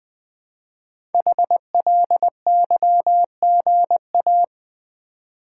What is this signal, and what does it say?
Morse code, heard as a keyed tone, about 20 wpm, 700 Hz. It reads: HLYGA
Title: Code Groups - 5 Letters - 20wpm